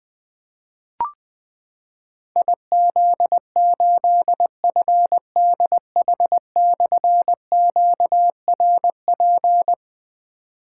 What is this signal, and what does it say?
Morse code, heard as a keyed tone, about 20 wpm, 700 Hz. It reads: IZ8FDH/QRP
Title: Difficult Call Signs 20wpm